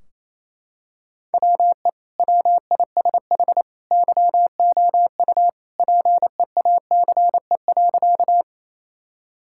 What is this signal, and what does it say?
Morse code, heard as a keyed tone, about 28 wpm, 700 Hz. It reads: WE WISH YOU PEACE.
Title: Sentences from Top 1600 Words 28wpm